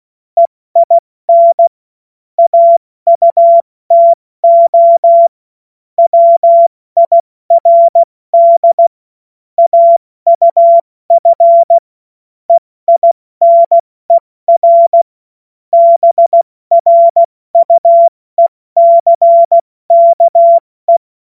Morse code, heard as a keyed tone, about 16 wpm, 680 Hz. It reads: EIN AUTO WIRD AUF EINER BRUECKE
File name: Morsekurs / Witze / Witz 26. Oktober 2021, 70 F80